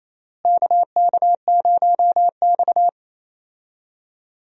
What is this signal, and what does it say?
Morse code, heard as a keyed tone, about 27 wpm, 700 Hz. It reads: KK0X
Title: USA Call Signs - 4 Characters - Repeated 3x - 28wpm